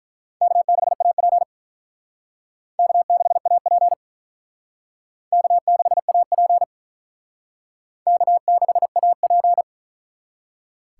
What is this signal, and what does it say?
Morse code, heard as a keyed tone, about 43 wpm, 700 Hz. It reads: K6AP K6AP K6AP K6AP
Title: Speed Racing - USA 1x2 Call Signs 35wpm